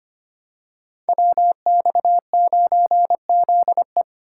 Morse code, heard as a keyed tone, about 25 wpm, 700 Hz. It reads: WX9ZE
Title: Code Groups - 5 Letter:Numbers - 25wpm